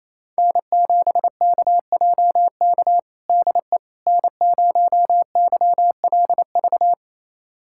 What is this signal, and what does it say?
Morse code, heard as a keyed tone, about 28 wpm, 700 Hz. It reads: N7KJK DE N0YLV
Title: USA Callsign DE Callsign - 5 Characters - 28wpm